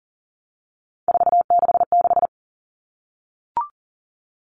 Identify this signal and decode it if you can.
Morse code, heard as a keyed tone, about 40 wpm, 700 Hz. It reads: 466
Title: Numbers - 3-Digit 40wpm